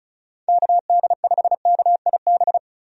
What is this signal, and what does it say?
Morse code, heard as a keyed tone, about 34 wpm, 700 Hz. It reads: KD5KIB
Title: USA Call Signs - 6 Characters - 35wpm